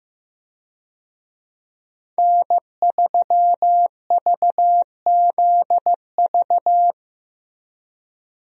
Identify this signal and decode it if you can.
Morse code, heard as a keyed tone, about 15 wpm, 700 Hz. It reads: N3VZV
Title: USA Call Signs - 5 Characters - Repeated 2x - 15wpm